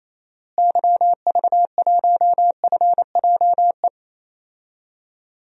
Morse code, heard as a keyed tone, about 28 wpm, 700 Hz. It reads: YV1FJE
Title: DXCC-100 Call Signs - 6 Characters - Repeated 3x - 28wpm